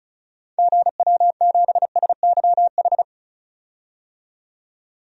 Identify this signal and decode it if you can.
Morse code, heard as a keyed tone, about 36 wpm, 700 Hz. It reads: GW7SYH
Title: DXCC-100 Call Signs - 6 Characters - 35wpm